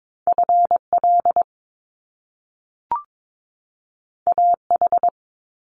Morse code, heard as a keyed tone, about 22 wpm, 700 Hz. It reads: FL AH
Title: Top Two Letter Combinations in Words 22wpm